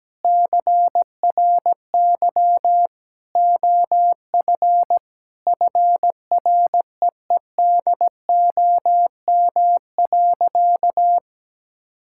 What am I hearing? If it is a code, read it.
Morse code, heard as a keyed tone, about 17 wpm, 700 Hz. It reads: CRY OF FREEDOM.
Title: Sentences from Top 1000 Words - Review Part 5 - 17wpm